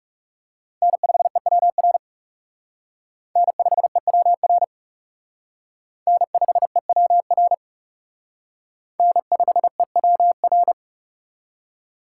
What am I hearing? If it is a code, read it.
Morse code, heard as a keyed tone, about 37 wpm, 700 Hz. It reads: N5EWR N5EWR N5EWR N5EWR
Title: Speed Racing - USA 1x3 Call Signs 30wpm